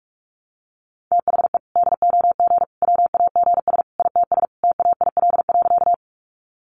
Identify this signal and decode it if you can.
Morse code, heard as a keyed tone, about 44 wpm, 700 Hz. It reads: THE DOG WAGS ITS TAIL.
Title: Easy Phrases 45wpm